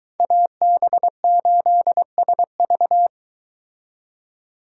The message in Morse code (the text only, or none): AB8SV